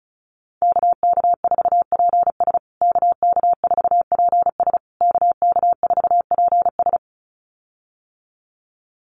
Morse code, heard as a keyed tone, about 34 wpm, 700 Hz. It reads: KK4PS KK4PS KK4PS
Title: USA Call Signs Repeated 3x 35wpm